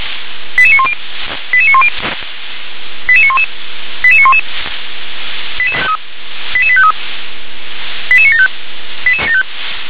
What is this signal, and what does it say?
Five-tone selective call (ZVEI), heard as quick runs of five tones, tones of about 70 ms. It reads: 88910 88910 88911 88911 88853 88975
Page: SDR/ZVEI